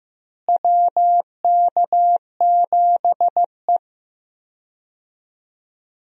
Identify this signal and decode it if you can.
Morse code, heard as a keyed tone, about 15 wpm, 700 Hz. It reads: WK7E
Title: USA Call Signs - 4 Characters - Repeated 3x - 15wpm